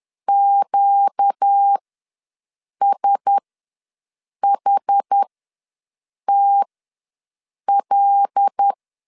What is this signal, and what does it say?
Morse code, heard as a keyed tone, about 10 wpm, spaced as for 6 wpm, 800 Hz. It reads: QSHTL
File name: LEZIONI / 5 LBVC / 5 LBV ws 10wpm 3645